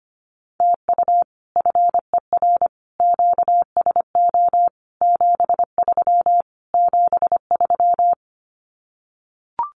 Morse code, heard as a keyed tone, about 25 wpm, 700 Hz. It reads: TU FER QSO 73 73
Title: Common QSO Phrases Encore 25wpm